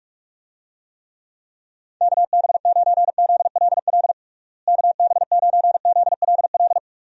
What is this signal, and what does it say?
Morse code, heard as a keyed tone, about 46 wpm, 700 Hz. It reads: KB9ZLL KB9ZLL
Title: USA Call Signs - 6 Characters - Repeated 3x - 45wpm